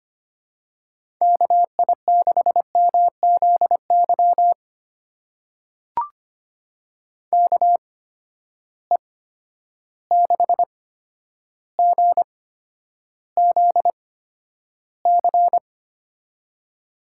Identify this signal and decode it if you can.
Morse code, heard as a keyed tone, about 25 wpm, spaced as for 11 wpm, 700 Hz. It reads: KI6MZY KE6GZC
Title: USA Call Signs - 6 Characters - 8x Character Spacing - 25wpm